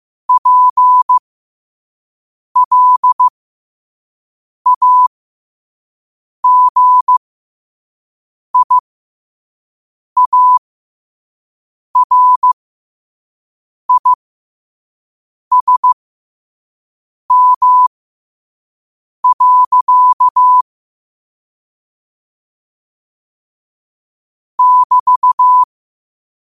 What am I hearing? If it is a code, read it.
Morse code, heard as a keyed tone, about 15 wpm, spaced as for 5 wpm, 1000 Hz. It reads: PLAGIARISM. <BT>
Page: Quotes for Tue, 12 Aug 2025 in Morse Code at 5 words per minute.